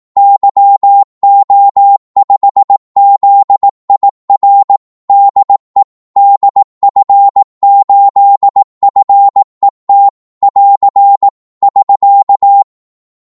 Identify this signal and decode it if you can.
Morse code, heard as a keyed tone, about 18 wpm, 800 Hz. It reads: YO5ZIR DE DF8FET <AR> <SK>
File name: Morse / qso-18wpm